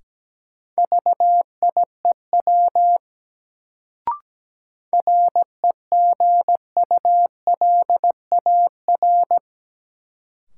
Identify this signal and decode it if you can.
Morse code, heard as a keyed tone, about 17 wpm, 700 Hz. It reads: VIEW REGULAR
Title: Sentences from Top 900 Words - New Word Drill - 17wpm